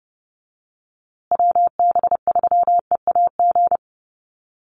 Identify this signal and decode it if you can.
Morse code, heard as a keyed tone, about 30 wpm, 700 Hz. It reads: WB3EAG
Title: USA Call Signs Repeated 3x 30wpm